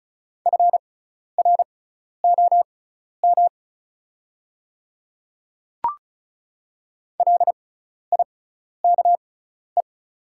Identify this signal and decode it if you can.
Morse code, heard as a keyed tone, about 36 wpm, spaced as for 12 wpm, 700 Hz. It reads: FROM LIKE
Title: 6x Character Spacing - Top 100 Words - 35wpm